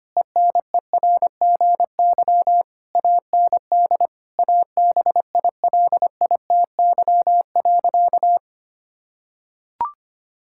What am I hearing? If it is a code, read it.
Morse code, heard as a keyed tone, about 25 wpm, 700 Hz. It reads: ENERGY AND ABILITY.
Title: Sentences from Top 1000 Words - Review Part 6 - 25wpm